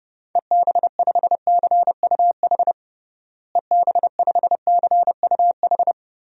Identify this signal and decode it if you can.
Morse code, heard as a keyed tone, about 30 wpm, 700 Hz. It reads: EB5CUH EB5CUH
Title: DXCC-100 Call Signs - 6 Characters - Repeated 3x - 30wpm